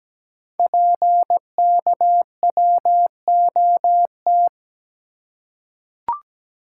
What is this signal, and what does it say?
Morse code, heard as a keyed tone, about 17 wpm, 700 Hz. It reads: PKWOT
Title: Code Groups - 5 Letters - 17wpm